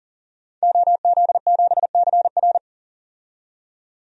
Morse code, heard as a keyed tone, about 38 wpm, 700 Hz. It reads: OZ7CR
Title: DXCC-100 Call Signs - 5 Characters - 40wpm